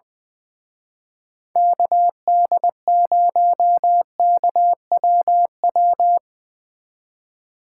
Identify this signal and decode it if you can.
Morse code, heard as a keyed tone, about 22 wpm, 700 Hz. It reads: KD0KWW
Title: Speed Racing - USA 2x3 Call Signs 20wpm